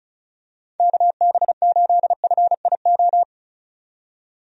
Morse code, heard as a keyed tone, about 34 wpm, 700 Hz. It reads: KB8FIO
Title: USA Call Signs - 6 Characters - 35wpm